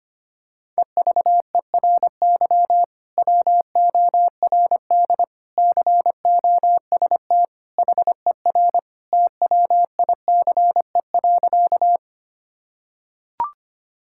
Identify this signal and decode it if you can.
Morse code, heard as a keyed tone, about 25 wpm, 700 Hz. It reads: EVERY WORD COST HER TWICE.
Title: Sentences from Top 1600 Words 25wpm